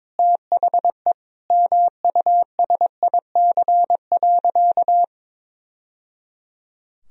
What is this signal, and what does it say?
Morse code, heard as a keyed tone, about 22 wpm, 700 Hz. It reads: THE MUSIC.
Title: Sentences from Top 1000 Words - Review Part 1 - 22wpm